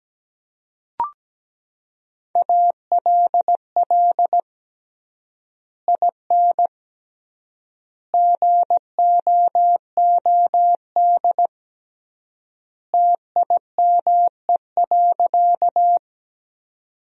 Morse code, heard as a keyed tone, about 17 wpm, 700 Hz. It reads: ALL IN GOOD TIME.
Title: 3x Word-Spacing - Sentences from Top 200 Words - 17wpm